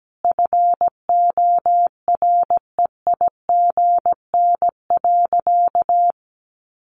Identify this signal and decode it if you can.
Morse code, heard as a keyed tone, about 17 wpm, 700 Hz. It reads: FOREIGN.
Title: Sentences from Top 1400 Words 17wpm